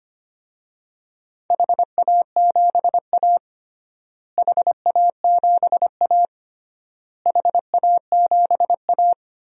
Morse code, heard as a keyed tone, about 25 wpm, 700 Hz. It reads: HA7A HA7A HA7A
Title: DXCC-100 Call Signs - 4 Characters - Repeated 3x - 25wpm